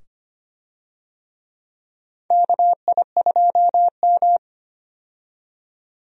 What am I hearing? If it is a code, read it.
Morse code, heard as a keyed tone, about 25 wpm, 700 Hz. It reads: KI2M
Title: USA Call Signs - 4 Characters - 25wpm